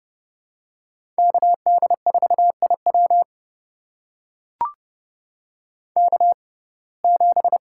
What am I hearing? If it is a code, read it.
Morse code, heard as a keyed tone, about 30 wpm, spaced as for 22 wpm, 700 Hz. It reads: KD4IW K7
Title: USA Call Signs - 5 Characters - 6x Character Spacing - 30wpm